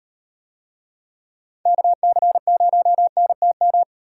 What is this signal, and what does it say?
Morse code, heard as a keyed tone, about 36 wpm, 700 Hz. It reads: KC0NTM